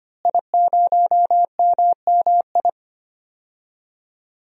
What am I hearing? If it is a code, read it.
Morse code, heard as a keyed tone, about 25 wpm, 700 Hz. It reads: I0MMI
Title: DXCC-100 Call Signs - 5 Characters - Repeated 3x - 25wpm